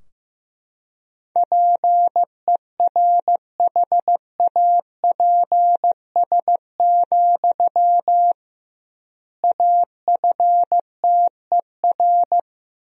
Morse code, heard as a keyed tone, about 15 wpm, 700 Hz. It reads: PERHAPS, AFTER
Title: 2x Word-Spacing - Sentences from Top 500 Words - 15wpm